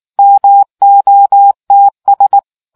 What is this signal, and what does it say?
Morse code, heard as a keyed tone, about 19 wpm, 800 Hz. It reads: MOTS